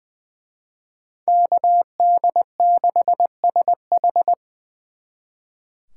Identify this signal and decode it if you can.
Morse code, heard as a keyed tone, about 24 wpm, 700 Hz. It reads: KD6SH
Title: Speed Racing - USA 2x2 Call Signs 20wpm